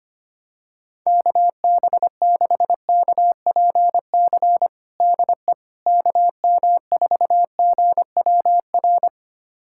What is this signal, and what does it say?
Morse code, heard as a keyed tone, about 25 wpm, 700 Hz. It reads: KB6KPC DE KM4GWR
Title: USA Callsign DE Callsign - 6 Characters - 25wpm